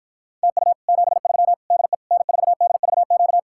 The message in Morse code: TU 73 DE N4DVQ

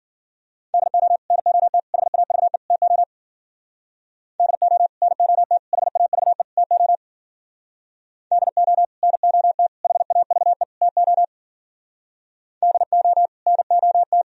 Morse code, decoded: DO NOT HAVE TO DO NOT HAVE TO DO NOT HAVE TO DO NOT

